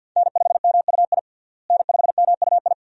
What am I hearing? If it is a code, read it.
Morse code, heard as a keyed tone, about 48 wpm, 700 Hz. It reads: N5MUI N5MUI
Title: USA Call Signs - 5 Characters - Repeated 2x - 50wpm